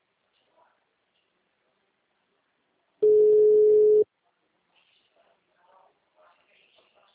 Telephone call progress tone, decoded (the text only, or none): ringback tone